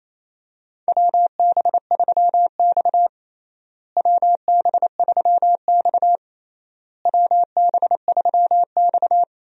Morse code, heard as a keyed tone, about 28 wpm, 700 Hz. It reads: WB3X WB3X WB3X
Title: USA Call Signs - 4 Characters - Repeated 3x - 28wpm